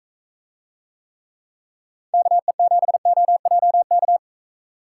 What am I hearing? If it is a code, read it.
Morse code, heard as a keyed tone, about 40 wpm, 700 Hz. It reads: KE7OJK